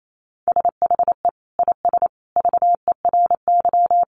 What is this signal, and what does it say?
Morse code, heard as a keyed tone, about 28 wpm, 700 Hz. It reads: SHE IS VERY
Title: Sentences from Top 400 Words 28wpm